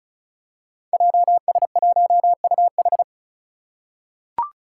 Morse code, heard as a keyed tone, about 36 wpm, 700 Hz. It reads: JS1UH
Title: Five Challenging Letter-Number Combinations 35wpm